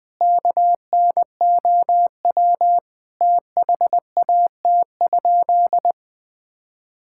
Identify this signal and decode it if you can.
Morse code, heard as a keyed tone, about 20 wpm, 700 Hz. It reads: KNOW THAT?
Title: Sentences with 4-Word Phrases - 20wpm